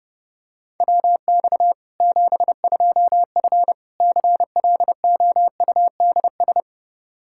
Morse code, heard as a keyed tone, about 30 wpm, 700 Hz. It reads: WX 72F CLOUDS
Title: K1USN SST QSO Practice 30wpm